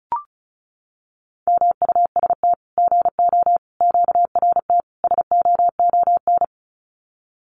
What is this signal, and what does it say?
Morse code, heard as a keyed tone, about 34 wpm, 700 Hz. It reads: MUST GO QRT SOON
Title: Common QSO Phrases 35wpm